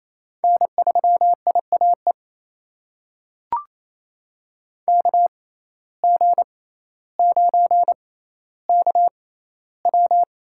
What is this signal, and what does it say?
Morse code, heard as a keyed tone, about 28 wpm, spaced as for 14 wpm, 700 Hz. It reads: N3IAE KG9KW
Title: USA Call Signs - 5 Characters - 6x Character Spacing - 28wpm